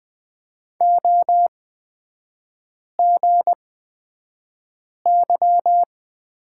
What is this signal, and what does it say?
Morse code, heard as a keyed tone, about 20 wpm, spaced as for 5 wpm, 700 Hz. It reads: OGY
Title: 224 Lesson 36e - BT - Three Characters 20wpm